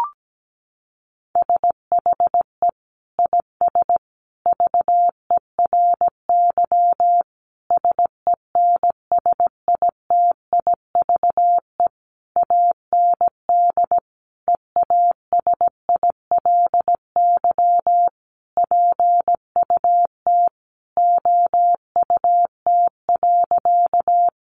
Morse code, heard as a keyed tone, about 17 wpm, 700 Hz. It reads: SHE IS VERY SENSITIVE AND EASILY PUT OUT.
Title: Sentences from Top 1400 Words 17wpm